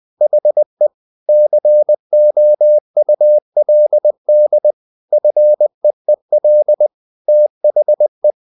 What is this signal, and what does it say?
Morse code, heard as a keyed tone, about 20 wpm, 600 Hz. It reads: HE COULD FEEL THE
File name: the old man and the sea / the old man and the sea part 1 20wpm